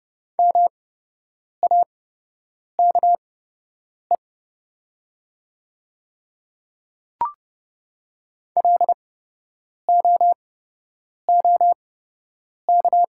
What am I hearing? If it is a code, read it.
Morse code, heard as a keyed tone, about 31 wpm, spaced as for 8 wpm, 700 Hz. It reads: MAKE LOOK
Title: 8x Character Spacing - Top 100 Words - 30wpm